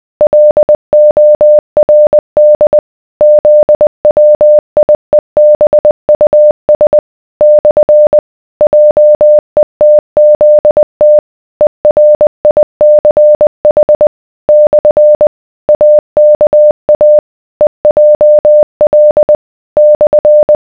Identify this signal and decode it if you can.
Morse code, heard as a keyed tone, about 20 wpm, 600 Hz. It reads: LORD ZWIEBUS / JETZT ERICH / AKA EJL /